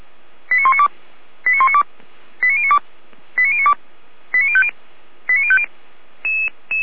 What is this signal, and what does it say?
Five-tone selective call (ZVEI), heard as quick runs of five tones, tones of about 80 ms. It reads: 89192 89192 89092 89092 89050 89050